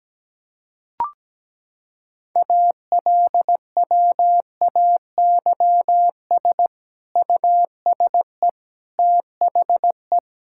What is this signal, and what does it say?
Morse code, heard as a keyed tone, about 17 wpm, 700 Hz. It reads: ALWAYS USE THE
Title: Sentences from Top 1000 Words - Review Part 6 - 17wpm